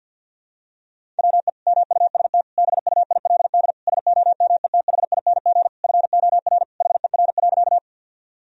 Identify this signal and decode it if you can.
Morse code, heard as a keyed tone, about 48 wpm, 700 Hz. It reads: WE MUST BUILD SOMETHING FOR HER.
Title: Sentences from Top 1000 Words - Review Part 1 - 50wpm